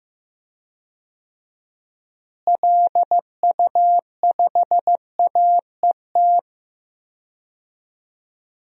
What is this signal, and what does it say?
Morse code, heard as a keyed tone, about 15 wpm, 700 Hz. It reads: LU5AET
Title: DXCC-100 Call Signs - 6 Characters - Repeated 2x - 15wpm